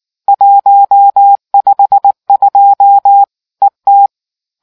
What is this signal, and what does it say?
Morse code, heard as a keyed tone, about 19 wpm, 800 Hz. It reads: 152 ET